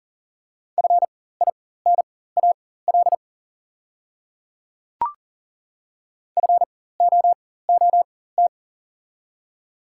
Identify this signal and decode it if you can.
Morse code, heard as a keyed tone, about 40 wpm, spaced as for 19 wpm, 700 Hz. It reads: FINAL FOOT
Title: 4x Character Spacing - Top 500 Words - 40wpm